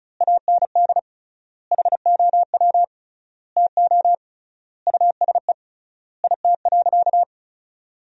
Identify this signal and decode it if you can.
Morse code, heard as a keyed tone, about 36 wpm, 700 Hz. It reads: AND HOW TO USE IT.
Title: 3x Word-Spacing - Sentences from Top 500 Words - 35wpm